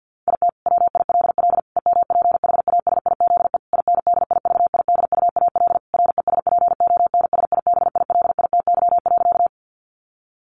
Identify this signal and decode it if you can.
Morse code, heard as a keyed tone, about 50 wpm, 700 Hz. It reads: IT WILL EMPHASIZE INDIVIDUAL RESPONSIBILITY.
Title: Sentences from Top 1500 Words 50wpm